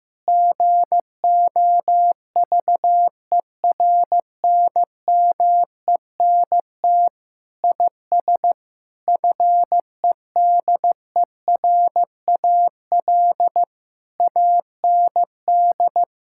Morse code, heard as a keyed tone, about 16 wpm, 700 Hz. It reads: GOVERNMENT IS FEDERAL AND
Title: Sentences from Top 1000 Words - Review Part 5 - 15wpm